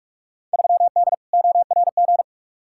Morse code, heard as a keyed tone, about 46 wpm, 700 Hz. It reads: 3D ORG
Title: Field Day Exchanges Encore - 45wpm